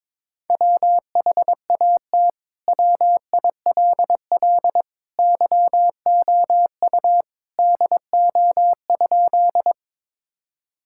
Morse code, heard as a keyed tone, about 22 wpm, 700 Hz. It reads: WHAT WILL YOU DO?
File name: practice / 22wpm / Sentences with 2-Word Phrases - 22wpm